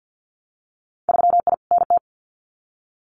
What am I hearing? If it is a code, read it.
Morse code, heard as a keyed tone, about 50 wpm, 700 Hz. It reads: 3I NT
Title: Winter Field Day Exchanges Encore 50wpm